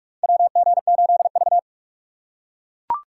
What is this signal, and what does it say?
Morse code, heard as a keyed tone, about 43 wpm, 700 Hz. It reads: WG8V
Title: USA Call Signs - 4 Characters - 2x Character Spacing - 45wpm